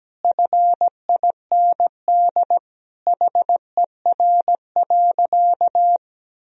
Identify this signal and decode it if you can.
Morse code, heard as a keyed tone, about 17 wpm, 700 Hz. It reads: FIND HER.
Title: Sentences from Top 1000 Words - Review Part 2 - 17wpm